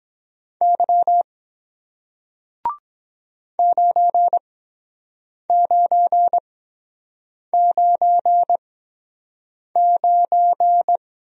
Morse code, heard as a keyed tone, about 23 wpm, spaced as for 7 wpm, 700 Hz. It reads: Y 9999